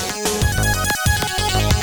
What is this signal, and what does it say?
There is no code.